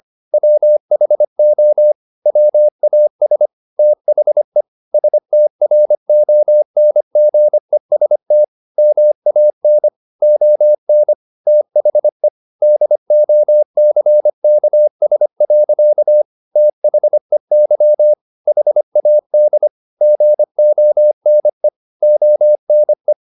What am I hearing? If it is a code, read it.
Morse code, heard as a keyed tone, about 25 wpm, 600 Hz. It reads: WHO WAS THE STRONGEST MAN ON THE DOCKS. THEY HAD GONE ONE
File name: the old man and the sea part 1 25wpm